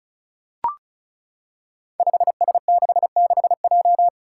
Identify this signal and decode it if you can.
Morse code, heard as a keyed tone, about 36 wpm, 700 Hz. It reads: 5S66J
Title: Five Challenging Letter-Number Combinations 35wpm